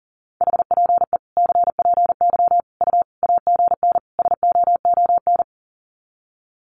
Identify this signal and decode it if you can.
Morse code, heard as a keyed tone, about 40 wpm, 700 Hz. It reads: HPE CPY U AGN SOON
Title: Common QSO Phrases 40wpm